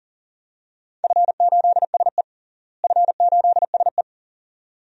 Morse code, heard as a keyed tone, about 40 wpm, 700 Hz. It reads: F8SE F8SE
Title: DXCC-100 Call Signs - 4 Characters - Repeated 2x - 40wpm